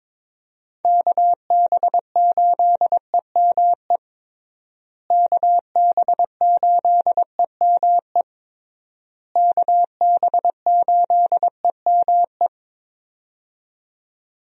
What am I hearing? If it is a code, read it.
Morse code, heard as a keyed tone, about 22 wpm, 700 Hz. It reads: KB8EME KB8EME KB8EME
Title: USA Call Signs - 6 Characters - Repeated 3x - 22wpm